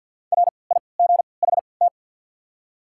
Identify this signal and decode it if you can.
Morse code, heard as a keyed tone, about 48 wpm, 690 Hz. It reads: R I G H T
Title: 3x Character Spacing - Top 200 Words - 50wpm